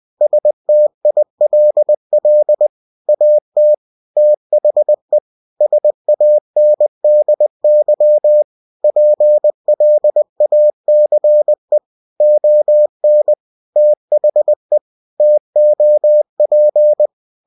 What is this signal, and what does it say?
Morse code, heard as a keyed tone, about 20 wpm, 600 Hz. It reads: STILL AT THE SANDY PLACE ON THE TOP
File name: cw-books / winnie-the-pooh / winnie-20wpm-page-0071